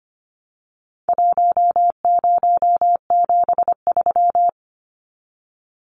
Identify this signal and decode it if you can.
Morse code, heard as a keyed tone, about 25 wpm, 700 Hz. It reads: 1073
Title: Numbers - 4-Digit 25wpm